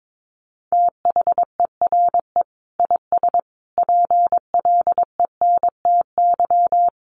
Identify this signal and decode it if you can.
Morse code, heard as a keyed tone, about 22 wpm, 700 Hz. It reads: THERE IS PLENTY
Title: Sentences from Top 1300 Words 22wpm